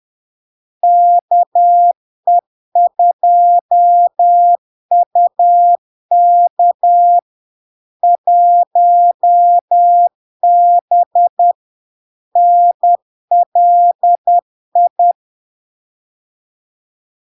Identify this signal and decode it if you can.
Morse code, heard as a keyed tone, about 10 wpm, 700 Hz. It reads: KE2UK 1B NLI
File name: LARK-FD-2022-Log-in-CW-10wpm-700-Hz